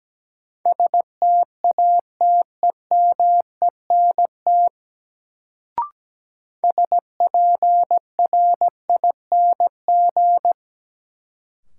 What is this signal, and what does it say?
Morse code, heard as a keyed tone, about 17 wpm, 700 Hz. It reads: STATEMENT SPRING
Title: Sentences from Top 1200 Words - New Word Drill Alt - 17wpm